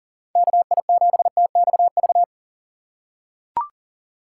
Morse code, heard as a keyed tone, about 37 wpm, 700 Hz. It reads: KI7TXV
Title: USA Call Signs - 6 Characters - 2x Character Spacing - 40wpm